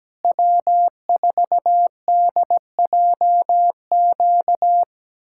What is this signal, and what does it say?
Morse code, heard as a keyed tone, about 17 wpm, 700 Hz. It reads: W4DJQ
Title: USA Callsign DE Callsign - 5 Characters - 17wpm